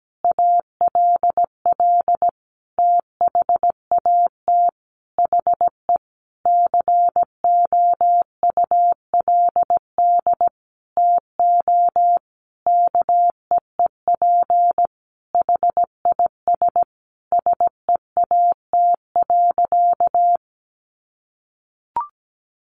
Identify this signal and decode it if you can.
Morse code, heard as a keyed tone, about 17 wpm, 700 Hz. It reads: ALL THAT HE COULD TO KEEP HIS SEAT.
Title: Sentences from Top 1500 Words 17wpm